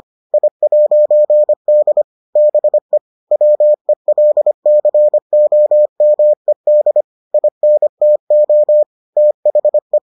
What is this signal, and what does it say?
Morse code, heard as a keyed tone, about 25 wpm, 600 Hz. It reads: I'D BE WELCOMED INTO THE